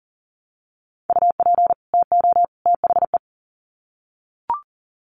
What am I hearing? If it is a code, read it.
Morse code, heard as a keyed tone, about 40 wpm, 700 Hz. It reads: UP TO THE
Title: Sets of 3 Words 40wpm